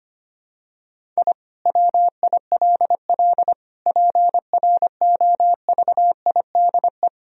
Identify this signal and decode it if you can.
Morse code, heard as a keyed tone, about 25 wpm, 700 Hz. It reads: I WILL PROVIDE
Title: Sentences from Top 1000 Words - Review Part 3 - 25wpm